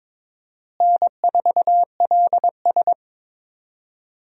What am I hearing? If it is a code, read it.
Morse code, heard as a keyed tone, about 22 wpm, 700 Hz. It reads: N4LS
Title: USA Call Signs - 4 Characters - Repeated 3x - 22wpm